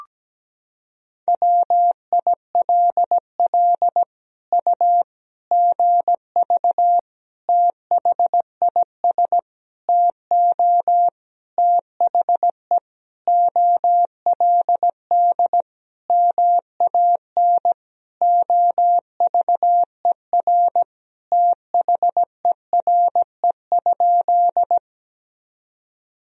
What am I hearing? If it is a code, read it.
Morse code, heard as a keyed tone, about 17 wpm, 700 Hz. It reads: WILL U GV THIS TO THE OLD MAN OVER THERE?
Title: Sentences from Top 200 Words and Abbreviations 17wpm